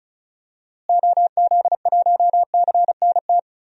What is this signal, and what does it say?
Morse code, heard as a keyed tone, about 36 wpm, 700 Hz. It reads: OZ1CNT
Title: DXCC-100 Call Signs - 6 Characters - Repeated 2x - 35wpm